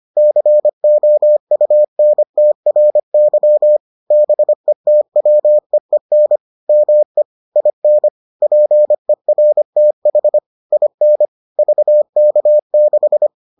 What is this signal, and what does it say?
Morse code, heard as a keyed tone, about 25 wpm, 600 Hz. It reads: COUNTRY BETWEEN ME IN PERTH IN VK6